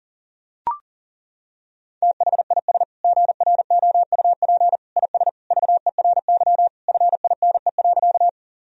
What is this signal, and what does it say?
Morse code, heard as a keyed tone, about 40 wpm, 700 Hz. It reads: THIS GROUP IS VERY FINE.
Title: Sentences from Top 1000 Words - Review Part 1 - 40wpm